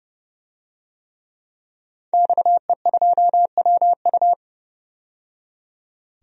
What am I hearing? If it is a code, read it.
Morse code, heard as a keyed tone, about 30 wpm, 700 Hz. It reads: XE2WU
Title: DXCC-100 Call Signs - 5 Characters - 30wpm